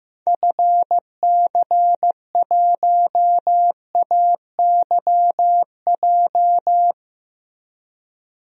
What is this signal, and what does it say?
Morse code, heard as a keyed tone, about 15 wpm, 700 Hz. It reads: FC1AYJ
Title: DXCC-100 Call Signs - 6 Characters - Repeated 3x - 15wpm